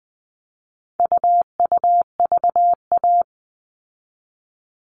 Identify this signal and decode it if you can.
Morse code, heard as a keyed tone, about 20 wpm, 700 Hz. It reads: UUVA
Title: All Cut Numbers - 4-Digit 20wpm